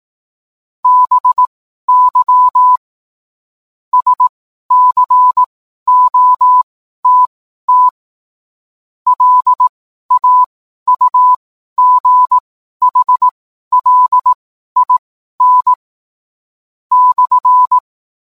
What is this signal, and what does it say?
Morse code, heard as a keyed tone, about 18 wpm, spaced as for 12 wpm, 1000 Hz. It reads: BY SCOTT LAUGHLIN /